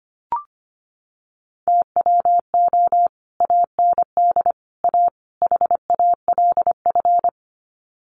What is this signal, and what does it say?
Morse code, heard as a keyed tone, about 25 wpm, 700 Hz. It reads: TWO AND A HALF
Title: Sets of 4 Words 25wpm